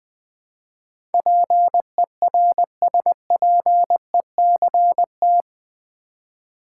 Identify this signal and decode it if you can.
Morse code, heard as a keyed tone, about 20 wpm, 700 Hz. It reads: PERSPECT
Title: Build A Word - 4 Letters At A Time - 11-Letter Words - 20wpm